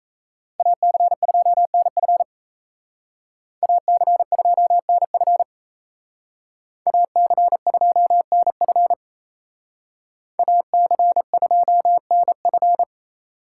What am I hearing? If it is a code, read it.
Morse code, heard as a keyed tone, about 32 wpm, 700 Hz. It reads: AC2NF AC2NF AC2NF AC2NF
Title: Speed Racing - USA 2x2 Call Signs 28wpm